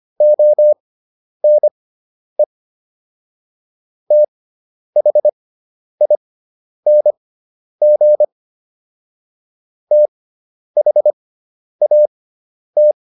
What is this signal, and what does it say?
Morse code, heard as a keyed tone, about 25 wpm, spaced as for 10 wpm, 600 Hz. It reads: ONE THING THAT